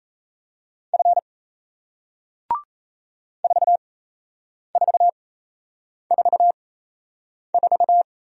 Morse code, heard as a keyed tone, about 38 wpm, spaced as for 8 wpm, 700 Hz. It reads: F 4444